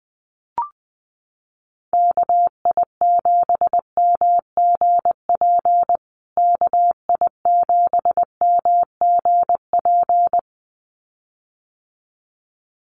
Morse code, heard as a keyed tone, about 20 wpm, 700 Hz. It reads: KI7MGP KI7MGP
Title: USA Call Signs Repeated 2x 20wpm